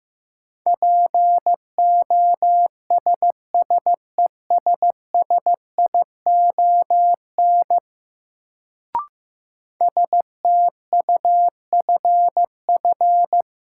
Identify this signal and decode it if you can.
Morse code, heard as a keyed tone, about 15 wpm, 700 Hz. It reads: POSSESSION STUFF
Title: Sentences from Top 1400 Words - New Word Drill Alt - 15wpm